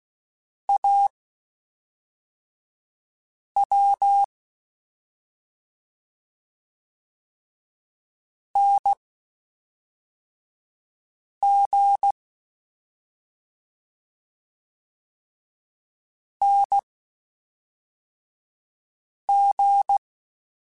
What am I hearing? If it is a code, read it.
Morse code, heard as a keyed tone, about 16 wpm, spaced as for 3 wpm, 800 Hz. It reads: AW NG NG
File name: Short Runs / Short Runs-mp3s / Unballanced @5